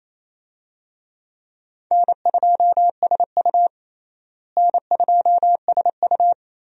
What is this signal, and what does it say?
Morse code, heard as a keyed tone, about 28 wpm, 700 Hz. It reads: N2SU N2SU
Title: USA Call Signs - 4 Characters - Repeated 3x - 28wpm